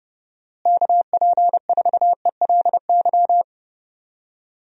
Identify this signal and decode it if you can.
Morse code, heard as a keyed tone, about 30 wpm, 700 Hz. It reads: KP4ELY
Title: USA Call Signs - 6 Characters - Repeated 3x - 30wpm